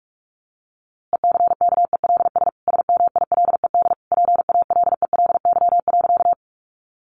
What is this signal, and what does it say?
Morse code, heard as a keyed tone, about 44 wpm, 700 Hz. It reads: ECKELS SMILED PALELY.
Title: A Sound Of Thunder 45wpm